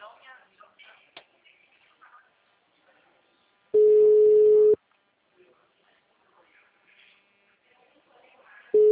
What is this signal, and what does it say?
Ringback tone (Europe, 425 Hz), with speech.